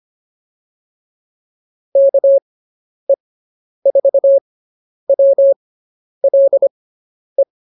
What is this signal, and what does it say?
Morse code, heard as a keyed tone, about 25 wpm, spaced as for 10 wpm, 550 Hz. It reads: KE4WLE